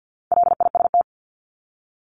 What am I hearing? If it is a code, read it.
Morse code, heard as a keyed tone, about 50 wpm, 700 Hz. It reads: LIST
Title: Top 500 Words 50wpm